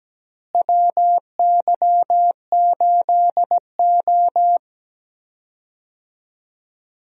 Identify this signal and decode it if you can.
Morse code, heard as a keyed tone, about 17 wpm, 700 Hz. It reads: WY8O
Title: USA Call Signs - 4 Characters - Repeated 3x - 17wpm